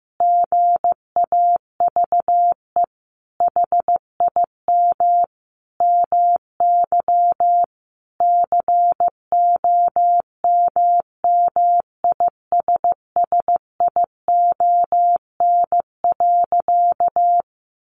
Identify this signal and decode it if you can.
Morse code, heard as a keyed tone, about 15 wpm, 700 Hz. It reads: GAVE HIM MY COMMISSION.
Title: Sentences from Top 1400 Words 15wpm